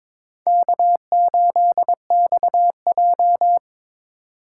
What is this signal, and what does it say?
Morse code, heard as a keyed tone, about 22 wpm, 700 Hz. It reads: K8XJ
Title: K1USN SST QSO Practice - 22wpm